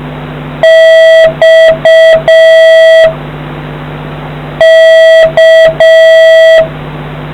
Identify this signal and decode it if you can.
Morse code, heard as a keyed tone, about 6 wpm, 650 Hz, with noise. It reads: X K